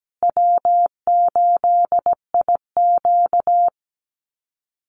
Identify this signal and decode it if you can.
Morse code, heard as a keyed tone, about 17 wpm, 700 Hz. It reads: W8IQ
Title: Call Signs Repeated 2x - 17wpm